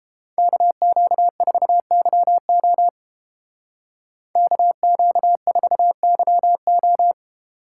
Morse code, heard as a keyed tone, about 29 wpm, 700 Hz. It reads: KQ4YO KQ4YO